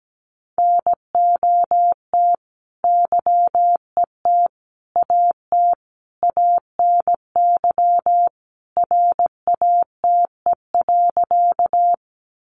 Morse code, heard as a keyed tone, about 17 wpm, 700 Hz. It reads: NOT YET AT ANY RATE.
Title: Sentences from Top 700 Words 17wpm